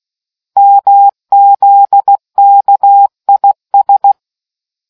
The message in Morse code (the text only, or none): MZKIS